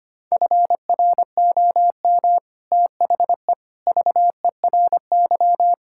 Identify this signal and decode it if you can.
Morse code, heard as a keyed tone, about 25 wpm, 700 Hz. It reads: FROM THE VERY
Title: Sentences from Top 1600 Words 25wpm